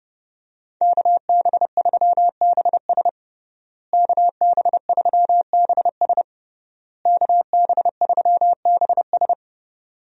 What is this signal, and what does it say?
Morse code, heard as a keyed tone, about 30 wpm, 700 Hz. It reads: KB3BS KB3BS KB3BS
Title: USA Call Signs - 5 Characters - Repeated 3x - 30wpm